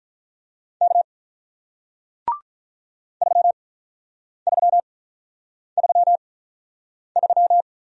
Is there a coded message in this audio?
Yes: Morse code from 0.8 to 7.6 s, heard as a keyed tone, about 48 wpm, spaced as for 9 wpm, 700 Hz. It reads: K 3333